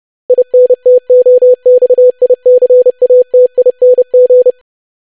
Morse code, heard as a keyed tone, about 30 wpm, 500 Hz. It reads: INTOXICATING
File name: lesson / web / longwords-25wpm